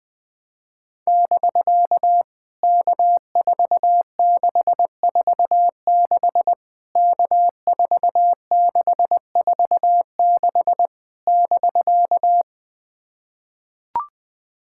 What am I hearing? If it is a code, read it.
Morse code, heard as a keyed tone, about 20 wpm, 700 Hz. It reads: <BK> K4646 K4646 <BK>
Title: POTA QSO Practice - 20wpm